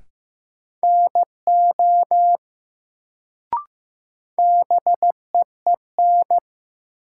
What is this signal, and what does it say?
Morse code, heard as a keyed tone, about 15 wpm, 700 Hz. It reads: NO BEEN
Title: Sentences from Top 100 Words - New Word Drill - 15wpm